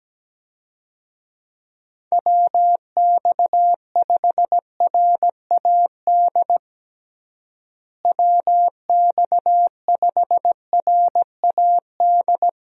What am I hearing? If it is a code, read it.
Morse code, heard as a keyed tone, about 17 wpm, 700 Hz. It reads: WX5RAD WX5RAD
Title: USA Call Signs - 6 Characters - Repeated 3x - 17wpm